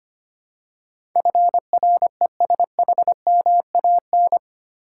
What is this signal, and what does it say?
Morse code, heard as a keyed tone, about 25 wpm, 700 Hz. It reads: FRESHMAN
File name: Eight Letter Words - 25wpm